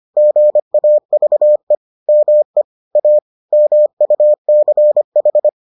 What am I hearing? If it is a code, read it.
Morse code, heard as a keyed tone, about 25 wpm, 600 Hz. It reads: GAVE ME A MUCH